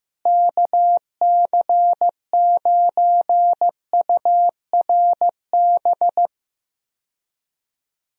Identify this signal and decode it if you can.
Morse code, heard as a keyed tone, about 15 wpm, 700 Hz. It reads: KC9URB
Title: USA Call Signs - 6 Characters - Repeated 2x - 15wpm